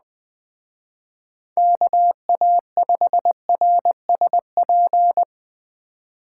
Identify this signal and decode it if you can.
Morse code, heard as a keyed tone, about 20 wpm, 700 Hz. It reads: KA5RSP
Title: Speed Racing - USA Call Signs 17wpm